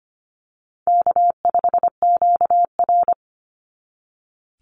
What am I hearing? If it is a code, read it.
Morse code, heard as a keyed tone, about 25 wpm, 700 Hz. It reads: K5QR
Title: NA SOTA Chaser Callsigns 25wpm